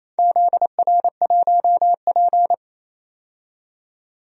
Morse code, heard as a keyed tone, about 28 wpm, 700 Hz. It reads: ZR1P
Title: DXCC-100 Call Signs - 4 Characters - 28wpm